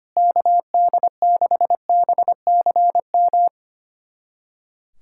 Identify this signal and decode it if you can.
Morse code, heard as a keyed tone, about 25 wpm, 700 Hz. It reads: KD6BCM
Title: USA Callsign DE Callsign - 6 Characters - 25wpm